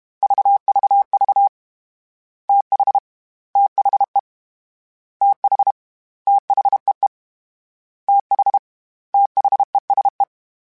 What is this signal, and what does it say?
Morse code, heard as a keyed tone, about 32 wpm, 800 Hz. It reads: VVV TH THE TH THEE TH THESE
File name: TH / TH mp3s / TH Prefixes 1x1 @30